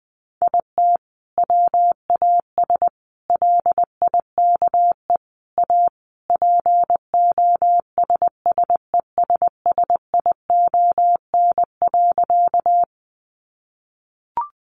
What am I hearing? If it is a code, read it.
Morse code, heard as a keyed tone, about 20 wpm, 700 Hz. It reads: IT WAS LIKE A POSSESSION.
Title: Sentences from Top 1400 Words 20wpm